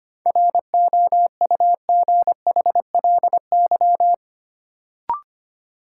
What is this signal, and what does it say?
Morse code, heard as a keyed tone, about 25 wpm, 700 Hz. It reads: ROUGHLY
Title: Sentences from Top 1500 Words - New Word Drill Alt - 25wpm